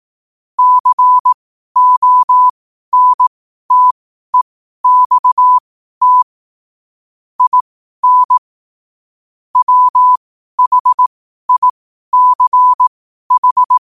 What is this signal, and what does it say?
Morse code, heard as a keyed tone, about 18 wpm, spaced as for 12 wpm, 1000 Hz. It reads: CONTEXT IN WHICH
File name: qotdfiles / 12wpm